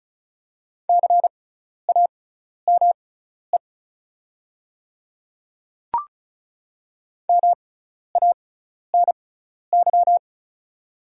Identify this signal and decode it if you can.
Morse code, heard as a keyed tone, about 36 wpm, spaced as for 12 wpm, 690 Hz. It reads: CAME MANY
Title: 6x Character Spacing - Top 200 Words - 35wpm